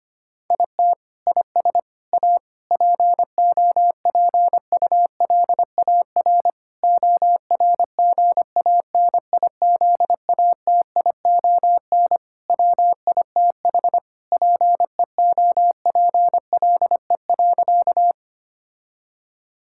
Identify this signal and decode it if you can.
Morse code, heard as a keyed tone, about 25 wpm, 700 Hz. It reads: IT IS A POPULAR ORGANIZATION WITH PEOPLE.
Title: Sentences from Top 1000 Words - Review Part 5 - 25wpm